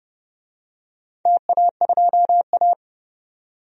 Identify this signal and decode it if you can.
Morse code, heard as a keyed tone, about 29 wpm, 700 Hz. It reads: TA2A
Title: DXCC-100 Call Signs - 4 Characters - 30wpm